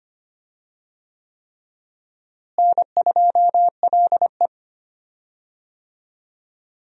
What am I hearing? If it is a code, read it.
Morse code, heard as a keyed tone, about 25 wpm, 700 Hz. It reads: N2LE
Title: USA Call Signs - 4 Characters - Repeated 3x - 25wpm